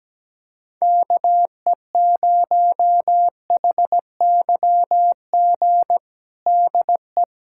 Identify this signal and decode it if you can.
Morse code, heard as a keyed tone, about 17 wpm, 700 Hz. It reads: KE0HYG DE
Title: USA Callsign DE Callsign - 6 Characters - 17wpm